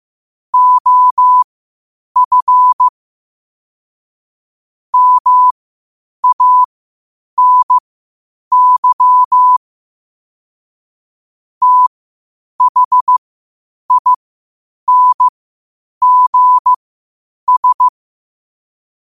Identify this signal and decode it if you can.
Morse code, heard as a keyed tone, about 15 wpm, spaced as for 8 wpm, 1000 Hz. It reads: OF MANY THINGS
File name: qotdfiles / 8wpm